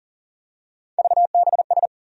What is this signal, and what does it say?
Morse code, heard as a keyed tone, about 40 wpm, 700 Hz. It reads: VBS
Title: USA Call Sign Postfixes - 3 Letters 40wpm